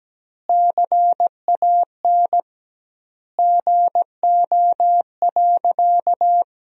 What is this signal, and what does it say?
Morse code, heard as a keyed tone, about 17 wpm, 700 Hz. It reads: CAN GO.
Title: 2x Word-Spacing - Sentences from Top 100 Words - 17wpm